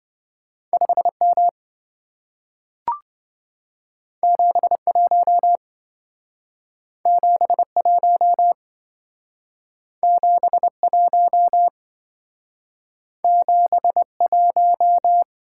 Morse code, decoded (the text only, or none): 5M 71 71 71 71